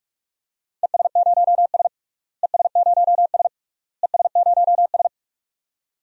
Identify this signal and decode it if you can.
Morse code, heard as a keyed tone, about 43 wpm, 700 Hz. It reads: ES0S ES0S ES0S
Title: DXCC-100 Call Signs - 4 Characters - Repeated 3x - 45wpm